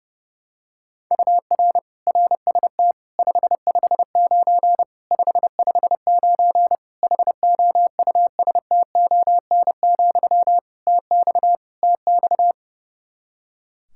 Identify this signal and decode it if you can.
Morse code, heard as a keyed tone, about 30 wpm, 700 Hz. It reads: UR RST 559 559 HOUSTON, TX TX
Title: K1USN SST QSO Practice 30wpm